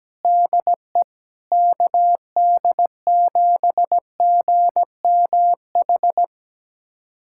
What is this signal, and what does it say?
Morse code, heard as a keyed tone, about 17 wpm, 700 Hz. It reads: DE KD7GMH
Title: USA Callsign DE Callsign - 6 Characters - 17wpm